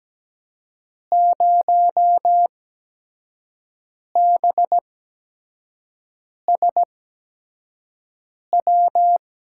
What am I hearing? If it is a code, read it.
Morse code, heard as a keyed tone, about 17 wpm, spaced as for 5 wpm, 700 Hz. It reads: 0BSW
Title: USA Call Signs - 6 Characters - 8x Character Spacing - 17wpm